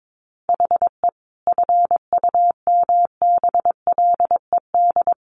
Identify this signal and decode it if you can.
Morse code, heard as a keyed tone, about 22 wpm, 700 Hz. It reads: HE FUMBLED
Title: A Sound Of Thunder 22wpm